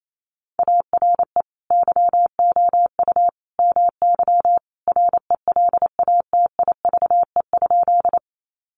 Morse code, heard as a keyed tone, about 28 wpm, 700 Hz. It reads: ARE YOU MY RELATIVE?
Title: Sentences from Top 1400 Words 28wpm